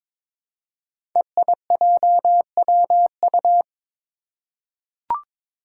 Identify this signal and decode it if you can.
Morse code, heard as a keyed tone, about 23 wpm, 700 Hz. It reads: EIJWU
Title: Five Challenging Letter-Number Combinations 22wpm